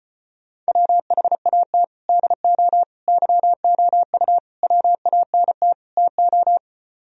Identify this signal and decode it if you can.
Morse code, heard as a keyed tone, about 31 wpm, 700 Hz. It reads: WHAT DO YOU WANT TO
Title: Speed-Racing - Sets of 5 Words 25wpm